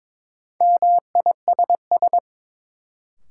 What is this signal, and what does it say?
Morse code, heard as a keyed tone, about 22 wpm, 700 Hz. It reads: MISS
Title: Sentences from Top 500 Words - New Word Drill - 22wpm